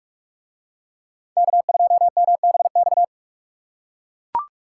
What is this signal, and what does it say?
Morse code, heard as a keyed tone, about 44 wpm, 700 Hz. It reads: K2MBX